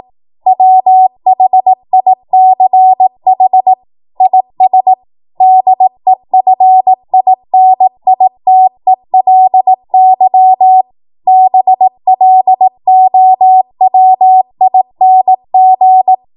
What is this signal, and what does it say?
Morse code, heard as a keyed tone, about 18 wpm, 750 Hz. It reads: WHICH IS DEFINITELY BLOWING
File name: Morse / 18WPM